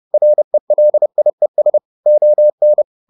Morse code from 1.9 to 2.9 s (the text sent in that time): ON